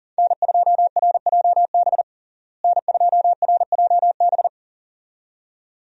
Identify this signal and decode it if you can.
Morse code, heard as a keyed tone, about 37 wpm, 700 Hz. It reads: N2RJB N2RJB
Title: USA Call Signs - 5 Characters - Repeated 2x - 40wpm